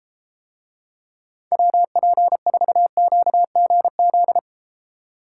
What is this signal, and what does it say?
Morse code, heard as a keyed tone, about 30 wpm, 700 Hz. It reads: WP4QGZ